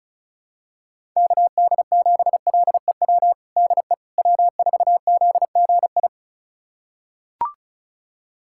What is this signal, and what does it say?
Morse code, heard as a keyed tone, about 36 wpm, 700 Hz. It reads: KD7LEW DE W4ZGI
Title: USA Callsign DE Callsign 35wpm